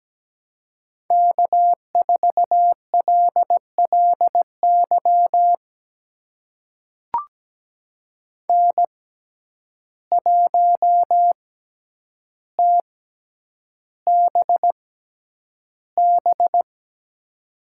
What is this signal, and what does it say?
Morse code, heard as a keyed tone, about 17 wpm, spaced as for 9 wpm, 700 Hz. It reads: K4LLY N1TBB
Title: USA Call Signs - 5 Characters - 6x Character Spacing - 17wpm